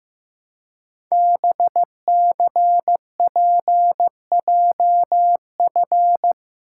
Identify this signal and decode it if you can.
Morse code, heard as a keyed tone, about 15 wpm, 700 Hz. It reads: BCPJF
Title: Code Groups - 5 Letters - 15wpm